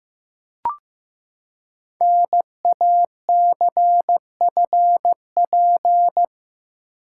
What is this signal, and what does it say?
Morse code, heard as a keyed tone, about 15 wpm, 700 Hz. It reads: NACFP
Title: Code Groups - 5 Letter:Numbers - 15wpm